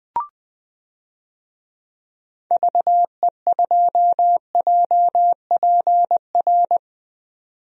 Morse code, heard as a keyed tone, about 21 wpm, 700 Hz. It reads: VE2JPR
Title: DXCC-100 Call Signs - 6 Characters - 20wpm